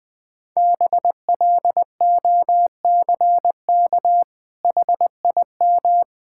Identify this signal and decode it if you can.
Morse code, heard as a keyed tone, about 21 wpm, 700 Hz. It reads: BLOCK HIM
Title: Sentences from Top 1600 Words 20wpm